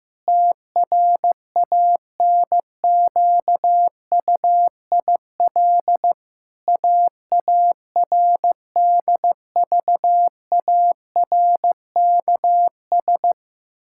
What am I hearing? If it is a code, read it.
Morse code, heard as a keyed tone, about 15 wpm, 700 Hz. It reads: TRANQUIL AARDVARKS
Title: Pangram Challenge v2 - 15wpm